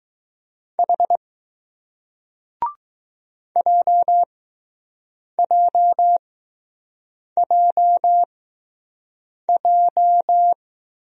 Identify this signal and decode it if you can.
Morse code, heard as a keyed tone, about 20 wpm, spaced as for 6 wpm, 700 Hz. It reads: H JJJJ